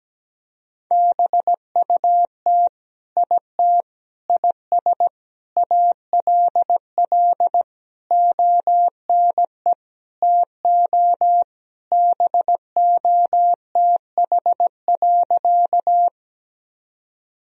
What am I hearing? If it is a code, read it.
Morse code, heard as a keyed tone, about 17 wpm, 700 Hz. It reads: BUT IT IS ALL ONE TO BOTH.
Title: Sentences from Top 1000 Words - Review Part 1 - 17wpm